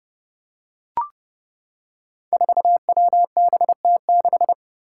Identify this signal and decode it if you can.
Morse code, heard as a keyed tone, about 30 wpm, 700 Hz. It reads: 4WBT6
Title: Five Challenging Letter-Number Combinations 30wpm